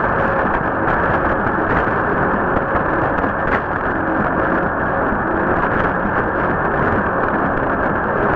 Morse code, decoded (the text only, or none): none